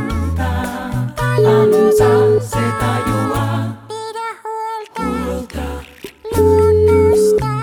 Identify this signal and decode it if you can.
Ringback tone (Europe, 425 Hz), with noise.